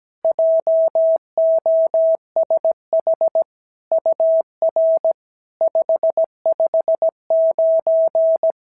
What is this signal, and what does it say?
Morse code, heard as a keyed tone, about 17 wpm, 650 Hz. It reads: JOSH UR 559
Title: SOTA QSO Practice 17wpm